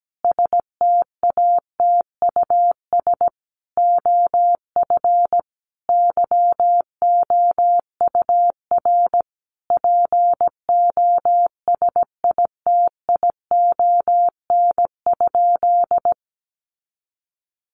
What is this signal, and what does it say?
Morse code, heard as a keyed tone, about 17 wpm, 700 Hz. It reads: STATUS OF YOUR POSITION?
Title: Sentences from Top 1400 Words 17wpm